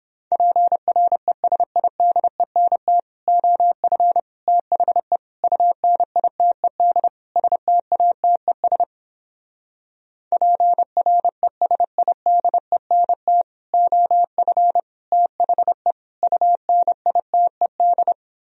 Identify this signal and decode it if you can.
Morse code, heard as a keyed tone, about 31 wpm, 700 Hz. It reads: PRESIDENT OF THE UNITED STATES PRESIDENT OF THE UNITED